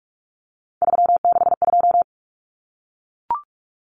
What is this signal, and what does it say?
Morse code, heard as a keyed tone, about 44 wpm, 700 Hz. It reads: 362
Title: Common Cut Numbers - 3-Digit 45wpm